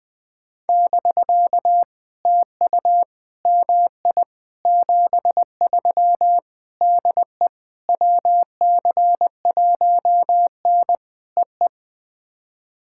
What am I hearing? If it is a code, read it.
Morse code, heard as a keyed tone, about 20 wpm, 700 Hz. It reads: <BK> TU MI 73 DE WC1N EE
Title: POTA QSO Practice - 20wpm